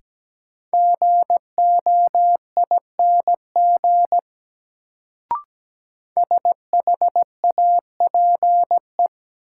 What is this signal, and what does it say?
Morse code, heard as a keyed tone, about 17 wpm, 700 Hz. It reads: GOING SHAPE
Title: Sentences from Top 500 Words - New Word Drill - 17wpm